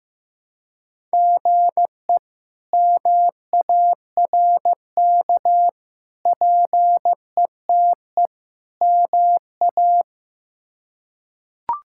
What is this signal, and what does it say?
Morse code, heard as a keyed tone, about 15 wpm, 700 Hz. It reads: GE MARK PETE MA
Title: K1USN SST QSO Practice - 15wpm